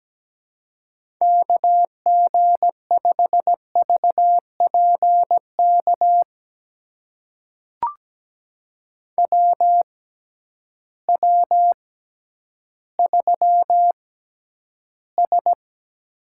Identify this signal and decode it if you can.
Morse code, heard as a keyed tone, about 17 wpm, spaced as for 10 wpm, 700 Hz. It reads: KG5VPK WW3S
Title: USA Call Signs - 6x Character Spacing - 17wpm